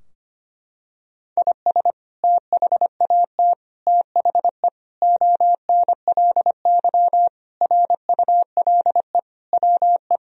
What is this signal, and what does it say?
Morse code, heard as a keyed tone, about 25 wpm, 700 Hz. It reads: IS THAT THE ONLY RULE WE
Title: Sentences from Top 1000 Words - Review Part 2 - 25wpm